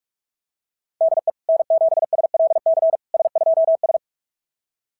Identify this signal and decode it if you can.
Morse code, heard as a keyed tone, about 44 wpm, 660 Hz. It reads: DE N7SLC S2S
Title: SOTA QSO Practice 45wpm